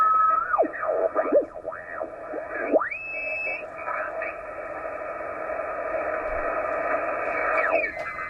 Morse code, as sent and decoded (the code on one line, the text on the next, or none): none